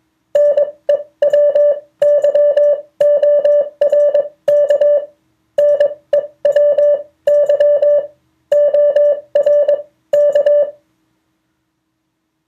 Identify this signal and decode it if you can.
Morse code, heard as a keyed tone, about 21 wpm, spaced as for 17 wpm, 580 Hz. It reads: NEWYORK NEWYORK